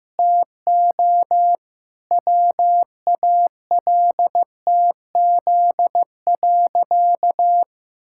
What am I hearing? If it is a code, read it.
Morse code, heard as a keyed tone, about 16 wpm, 700 Hz. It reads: TO WALTZ.
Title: Pangram Challenge v2 - 15wpm